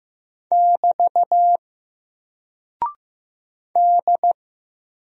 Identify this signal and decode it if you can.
Morse code, heard as a keyed tone, about 15 wpm, 700 Hz. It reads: <BT> D
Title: Single Character (Letter:Number:Punctuation:Prosign) 15wpm